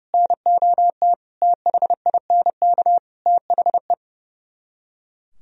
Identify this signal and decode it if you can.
Morse code, heard as a keyed tone, about 39 wpm, 700 Hz. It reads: NOT THINK THE